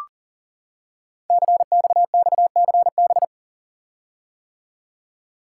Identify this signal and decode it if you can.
Morse code, heard as a keyed tone, about 42 wpm, 700 Hz. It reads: CXXCB
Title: Code Groups - 5 Letters - 40wpm